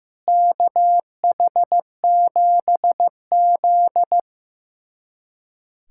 Morse code, heard as a keyed tone, about 16 wpm, 700 Hz. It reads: KH7Z
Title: USA Callsign DE Callsign - 4 Characters - 15wpm